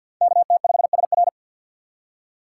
Morse code, heard as a keyed tone, about 51 wpm, 700 Hz. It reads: KT5SR
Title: USA Call Signs - 5 Characters - 2x Character Spacing - 50wpm